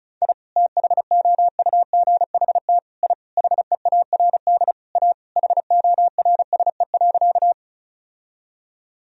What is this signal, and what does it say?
Morse code, heard as a keyed tone, about 36 wpm, 700 Hz. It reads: I THOUGHT I HEARD A HORSE.
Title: Sentences from Top 1000 Words - Review Part 1 - 35wpm